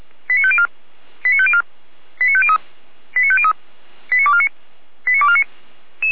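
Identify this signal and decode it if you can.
Five-tone selective call (ZVEI), heard as quick runs of five tones, tones of about 70 ms. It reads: 89504 89504 89503 89503 89249 89249
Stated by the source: ZVEI five-tone selective call received by SDR, logged by the Feuerwehr Orlamünde fire brigade server.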